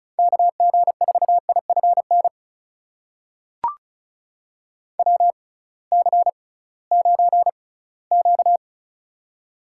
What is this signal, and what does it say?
Morse code, heard as a keyed tone, about 36 wpm, spaced as for 21 wpm, 700 Hz. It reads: KG4IFN WC9Q
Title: USA Call Signs - 6x Character Spacing - 35wpm